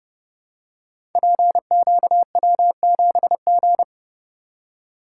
Morse code, heard as a keyed tone, about 30 wpm, 700 Hz. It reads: PQW7G
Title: Code Groups - 5 Letter:Numbers - 30wpm